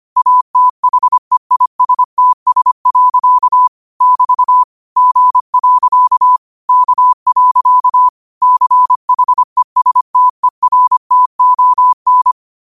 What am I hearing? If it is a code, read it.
Morse code, heard as a keyed tone, about 25 wpm, 1000 Hz. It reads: ATHEISTS. <BT> G. K. CHESTERTON